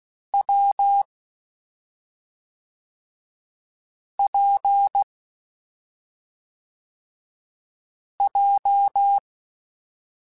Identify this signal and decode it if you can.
Morse code, heard as a keyed tone, about 16 wpm, 800 Hz. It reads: W P J
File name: Difficult Combos / Difficult Combos-mp3s / WPJ WPJ1 @5